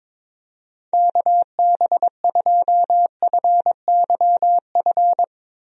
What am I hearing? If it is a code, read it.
Morse code, heard as a keyed tone, about 22 wpm, 700 Hz. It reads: KB2FYF
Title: USA Call Signs - 6 Characters - Repeated 3x - 22wpm